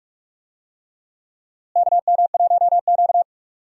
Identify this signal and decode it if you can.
Morse code, heard as a keyed tone, about 46 wpm, 700 Hz. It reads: KM1Q
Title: USA Call Signs - 4 Characters - Repeated 3x - 45wpm